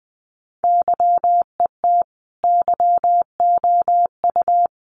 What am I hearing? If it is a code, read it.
Morse code, heard as a keyed tone, about 20 wpm, 700 Hz. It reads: YET YOU
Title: Sentences from Top 600 Words 20wpm